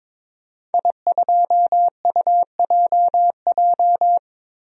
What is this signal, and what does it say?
Morse code, heard as a keyed tone, about 21 wpm, 700 Hz. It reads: I2UJJ
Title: DXCC-100 Call Signs - 5 Characters - 22wpm